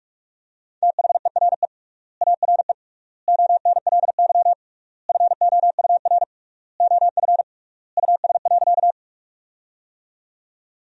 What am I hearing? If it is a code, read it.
Morse code, heard as a keyed tone, about 42 wpm, 700 Hz. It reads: THERE ARE ONLY FOUR OF US.
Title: 3x Word-Spacing - Sentences from Top 300 Words - 45wpm